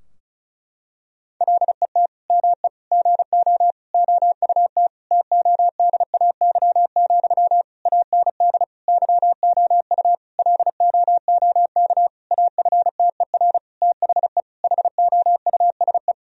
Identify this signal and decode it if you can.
Morse code, heard as a keyed tone, about 36 wpm, 700 Hz. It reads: LET ME GO OUT TODAY, AND YOU LOOK AFTER THE HOUSE
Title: Sentences from Top 1000 Words - Review Part 4 - 35wpm